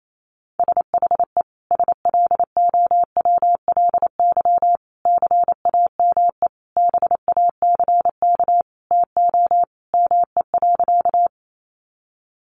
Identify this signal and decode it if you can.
Morse code, heard as a keyed tone, about 28 wpm, 700 Hz. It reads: SHE SLOWLY CAME BACK TO ME.
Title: Sentences from Top 1100 Words 28wpm